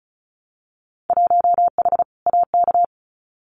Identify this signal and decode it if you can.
Morse code, heard as a keyed tone, about 36 wpm, 700 Hz. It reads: 1H AK
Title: Winter Field Day Exchanges Encore 35wpm